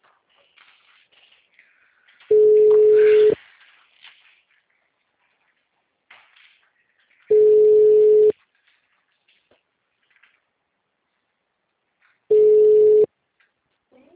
Ringback tone (Europe, 425 Hz).